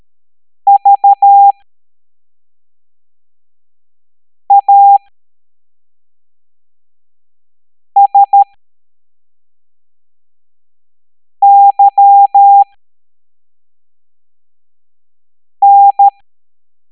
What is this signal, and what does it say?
Morse code, heard as a keyed tone, about 13 wpm, spaced as for 3 wpm, 800 Hz. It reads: VASYN